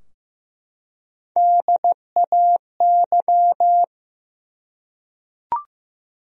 Morse code, heard as a keyed tone, about 15 wpm, 700 Hz. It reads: DAY
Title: Top 100 Words 15wpm